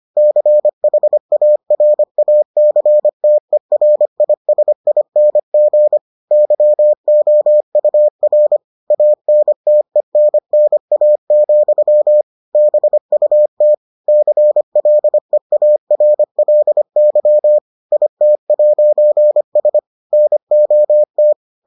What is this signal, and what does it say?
Morse code, heard as a keyed tone, about 25 wpm, 600 Hz. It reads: CHARACTERISING YOUR ANTENNA, BUT CLEARLY IT'S NOT